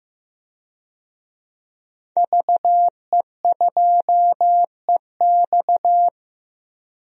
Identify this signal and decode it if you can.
Morse code, heard as a keyed tone, about 15 wpm, 700 Hz. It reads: VE2EX
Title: DXCC-100 Call Signs - 5 Characters - Repeated 2x - 15wpm